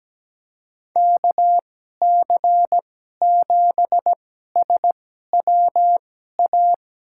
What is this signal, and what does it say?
Morse code, heard as a keyed tone, about 17 wpm, spaced as for 12 wpm, 700 Hz. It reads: KC7SWA
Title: USA Call Signs - 6 Characters - 2x Character Spacing - 17wpm